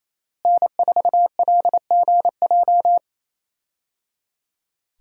Morse code, heard as a keyed tone, about 27 wpm, 700 Hz. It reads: N4LGJ
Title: USA Call Signs - 5 Characters - Repeated 3x - 28wpm